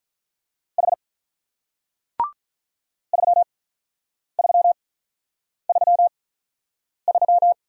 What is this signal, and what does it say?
Morse code, heard as a keyed tone, about 45 wpm, 700 Hz. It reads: H 3 3 3 3